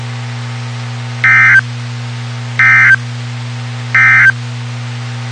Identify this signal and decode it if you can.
Emergency Alert System (SAME) tones, heard as harsh buzzing bursts. It is heard from 1.1 to 4.4 s.
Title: WXK91 NOAA Weather Radio